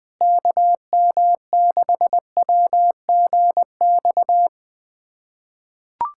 Morse code, heard as a keyed tone, about 20 wpm, 700 Hz. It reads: KM6WGX
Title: USA Callsign DE Callsign 20wpm